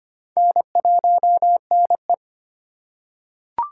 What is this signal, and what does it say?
Morse code, heard as a keyed tone, about 25 wpm, 700 Hz. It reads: N1NE
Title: USA Call Signs - 4 Characters - 2x Character Spacing - 25wpm